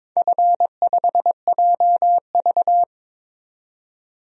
Morse code, heard as a keyed tone, about 21 wpm, 700 Hz. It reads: F5JV
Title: DXCC-100 Call Signs - 4 Characters - 22wpm